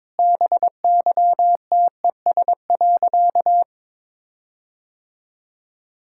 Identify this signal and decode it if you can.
Morse code, heard as a keyed tone, about 22 wpm, 690 Hz. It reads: BYTES.
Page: morse-code-ninja-practice-22wpm